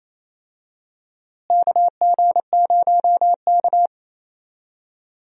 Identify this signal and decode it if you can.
Morse code, heard as a keyed tone, about 28 wpm, 700 Hz. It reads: KG0K
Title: USA Call Signs - 4 Characters - Repeated 2x - 28wpm